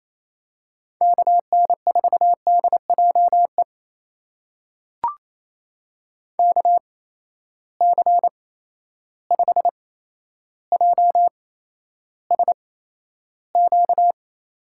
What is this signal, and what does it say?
Morse code, heard as a keyed tone, about 28 wpm, spaced as for 12 wpm, 700 Hz. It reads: KN4DJE KC5JSQ
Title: USA Call Signs - 6 Characters - 8x Character Spacing - 28wpm